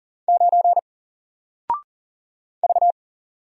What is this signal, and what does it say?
Morse code, heard as a keyed tone, about 40 wpm, 700 Hz. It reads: 9 V
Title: Single Letter-Number 40wpm